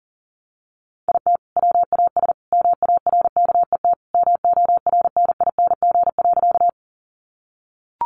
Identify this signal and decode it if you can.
Morse code, heard as a keyed tone, about 40 wpm, 700 Hz. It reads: IT WAS MARKET MORNING.
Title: Sentences from Top 600 Words 40wpm